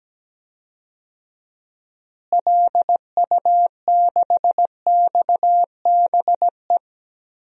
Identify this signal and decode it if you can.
Morse code, heard as a keyed tone, about 17 wpm, 700 Hz. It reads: LU6XBE
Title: DXCC-100 Call Signs - 6 Characters - Repeated 2x - 17wpm